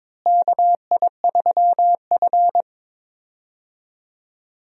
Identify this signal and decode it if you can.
Morse code, heard as a keyed tone, about 22 wpm, 700 Hz. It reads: KI3F
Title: USA Call Signs - 4 Characters - Repeated 2x - 22wpm